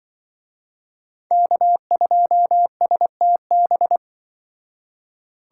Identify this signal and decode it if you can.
Morse code, heard as a keyed tone, about 25 wpm, 700 Hz. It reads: K2STB
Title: Speed Racing - USA 1x3 Call Signs 20wpm